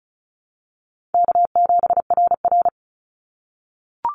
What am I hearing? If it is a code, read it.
Morse code, heard as a keyed tone, about 36 wpm, 700 Hz. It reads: K7RR
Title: Call Signs - 35wpm